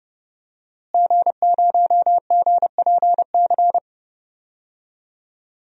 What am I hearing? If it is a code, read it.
Morse code, heard as a keyed tone, about 30 wpm, 700 Hz. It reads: G0GPC
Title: DXCC-100 Call Signs - 5 Characters - Repeated 3x - 30wpm